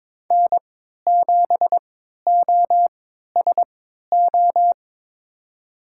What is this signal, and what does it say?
Morse code, heard as a keyed tone, about 22 wpm, spaced as for 12 wpm, 700 Hz. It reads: N7OSO
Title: USA Call Signs - 5 Characters - 3x Character Spacing - 22wpm